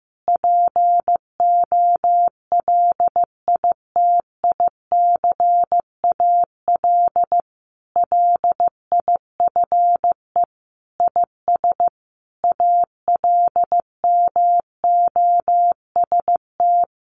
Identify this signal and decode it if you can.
Morse code, heard as a keyed tone, about 15 wpm, 700 Hz. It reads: POLITICAL LIFE IS ALMOST
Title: Sentences from Top 1500 Words 15wpm